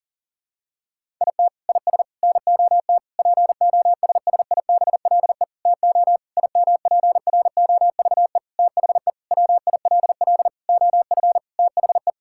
Morse code, heard as a keyed tone, about 40 wpm, 700 Hz. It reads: IT IS NOT POSSIBLE TO IMPROVE THE WILL OF THE
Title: Sentences from Top 1000 Words - Review Part 5 - 40wpm